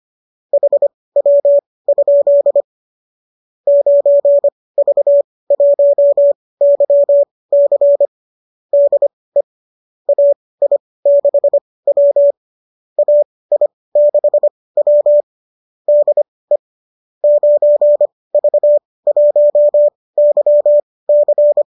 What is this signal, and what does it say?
Morse code, heard as a keyed tone, about 25 wpm, spaced as for 18 wpm, 570 Hz. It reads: HW? 9V1YC DE AI6W AI6W DE 9V1YC
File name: qso207 18